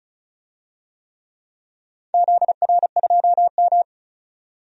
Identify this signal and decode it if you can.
Morse code, heard as a keyed tone, about 36 wpm, 700 Hz. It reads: ZR2M
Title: DXCC-100 Call Signs - 4 Characters - 35wpm